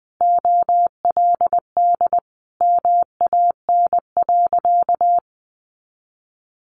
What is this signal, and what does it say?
Morse code, heard as a keyed tone, about 20 wpm, 700 Hz. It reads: OLD MAN.
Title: Sentences from Top 1100 Words 20wpm